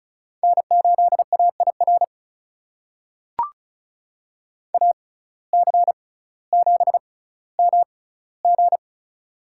Morse code, heard as a keyed tone, about 36 wpm, spaced as for 18 wpm, 700 Hz. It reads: N8AIR AC7MG
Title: USA Call Signs - 5 Characters - 6x Character Spacing - 35wpm